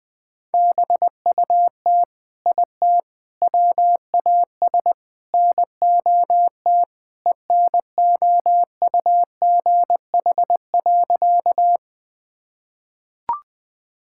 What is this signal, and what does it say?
Morse code, heard as a keyed tone, about 20 wpm, 700 Hz. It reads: BUT IT WAS NOT ENOUGH.
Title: Sentences with 4-Word Phrases - Spoken Beginning - 20wpm